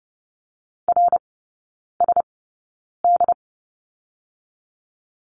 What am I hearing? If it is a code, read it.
Morse code, heard as a keyed tone, about 30 wpm, spaced as for 9 wpm, 700 Hz. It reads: RSD
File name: Lesson 18e - Letter m - Three Characters 30wpm